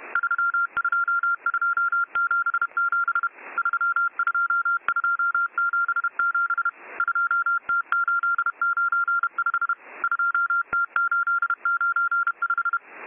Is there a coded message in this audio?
Yes: Morse code from 0.2 to 12.8 s, heard as a keyed tone, about 31 wpm, 1390 Hz. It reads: 32277 32277 2T895 2T895